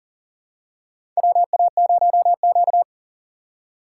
Morse code, heard as a keyed tone, about 37 wpm, 700 Hz. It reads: WA0Q
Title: USA Call Signs - 4 Characters - 2x Character Spacing - 40wpm